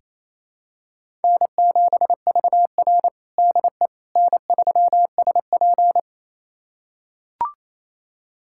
Morse code, heard as a keyed tone, about 28 wpm, 700 Hz. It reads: N7VR DE N3SP
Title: USA Callsign DE Callsign - 4 Characters - 28wpm